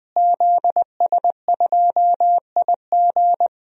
Morse code, heard as a keyed tone, about 21 wpm, 700 Hz. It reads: ZS2IG
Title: DXCC-100 Call Signs - 5 Characters - Repeated 3x - 20wpm